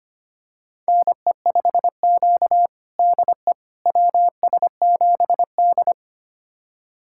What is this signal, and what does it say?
Morse code, heard as a keyed tone, about 25 wpm, 700 Hz. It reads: NE5Q DE WS7D
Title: USA Callsign DE Callsign - 4 Characters - 25wpm